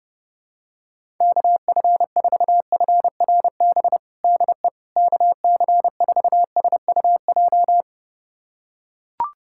Morse code, heard as a keyed tone, about 30 wpm, 700 Hz. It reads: KF4FRB DE KC4SUJ
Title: USA Callsign DE Callsign - 6 Characters - 30wpm